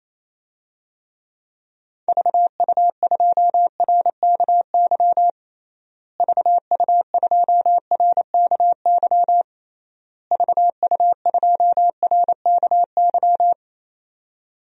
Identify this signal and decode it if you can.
Morse code, heard as a keyed tone, about 28 wpm, 700 Hz. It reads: VU2RKY VU2RKY VU2RKY
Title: DXCC-100 Call Signs - 6 Characters - Repeated 3x - 28wpm